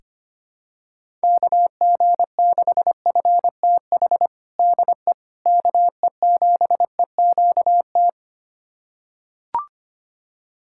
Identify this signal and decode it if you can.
Morse code, heard as a keyed tone, about 25 wpm, 700 Hz. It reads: KG6FTH DE KE7EQT
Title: USA Callsign DE Callsign - 6 Characters - 25wpm